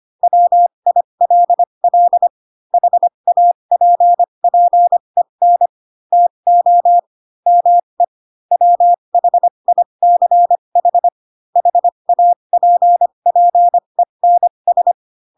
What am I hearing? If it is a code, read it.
Morse code, heard as a keyed tone, about 25 wpm, spaced as for 22 wpm, 700 Hz. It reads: WILL HAPPEN TO ME WHICH HAPPENS
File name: Pinocchio / Pinocchio ch4 18 25 ik1hge